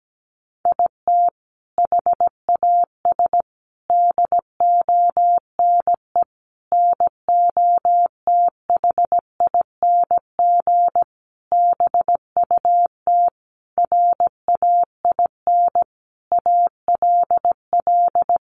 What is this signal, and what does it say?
Morse code, heard as a keyed tone, about 17 wpm, 700 Hz. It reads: IT HAS DONE NOTHING BUT RAIN ALL
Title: Sentences from Top 1100 Words 17wpm